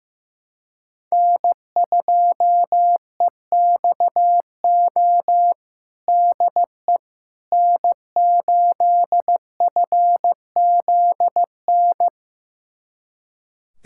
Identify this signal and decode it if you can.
Morse code, heard as a keyed tone, about 15 wpm, 700 Hz. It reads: N2EXO DE N8FZN
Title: USA Callsign DE Callsign - 5 Characters - 15wpm